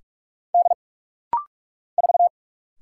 Morse code, heard as a keyed tone, about 46 wpm, 700 Hz. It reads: D 4
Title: Single Letter-Number - Rapid-Fire 45wpm